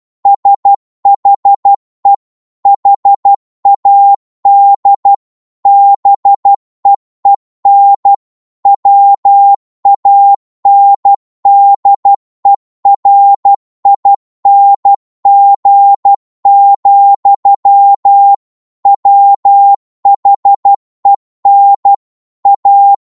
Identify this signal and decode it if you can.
Morse code, heard as a keyed tone, about 12 wpm, 800 Hz. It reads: SHE HAD BEEN WANDERING, WHEN A